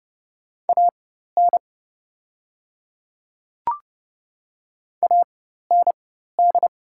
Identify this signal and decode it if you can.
Morse code, heard as a keyed tone, about 30 wpm, spaced as for 14 wpm, 700 Hz. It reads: AN AND
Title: 4x Character Spacing - Top 100 Words - 30wpm